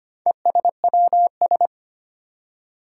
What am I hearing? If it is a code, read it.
Morse code, heard as a keyed tone, about 25 wpm, 700 Hz. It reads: ESWS
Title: Four Challenging Letter-Number Combinations 25wpm